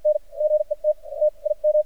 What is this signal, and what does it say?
Morse code, heard as a keyed tone, about 55 wpm, spaced as for 32 wpm, 610 Hz. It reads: NGE TTIM
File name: archived / CRSnet